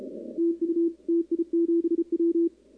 Morse code, heard as a keyed tone, about 27 wpm, 340 Hz, with radio noise. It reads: TU TI7W
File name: contests / 2019-cw-wpx-kt5j-k5pi- -2330 0-11